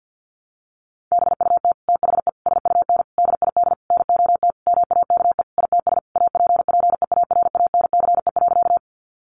Morse code, heard as a keyed tone, about 50 wpm, 700 Hz. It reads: BUT THE SUN DID NOT MAKE ITS APPEARANCE.
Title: Sentences from Top 1100 Words 50wpm